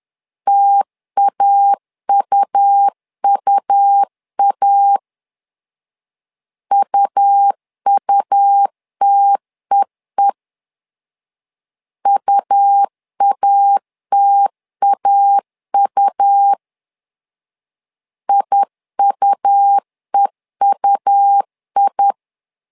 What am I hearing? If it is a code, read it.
Morse code, heard as a keyed tone, about 10 wpm, 800 Hz. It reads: TAUUA UUTEE UATAU IUEUI
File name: LEZIONI / 2 UHDN / 2 U 10wpm 1622